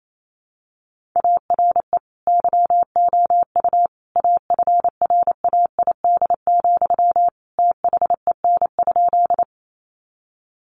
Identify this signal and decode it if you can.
Morse code, heard as a keyed tone, about 28 wpm, 700 Hz. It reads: ARE YOU AFRAID, THEN?
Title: Sentences from Top 1500 Words 28wpm